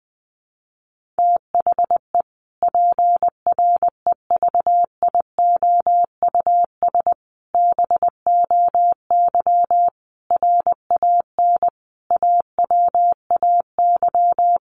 Morse code, heard as a keyed tone, about 20 wpm, 700 Hz. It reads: THE PREVIOUS BOY RAN AWAY
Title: Sentences from Top 1500 Words 20wpm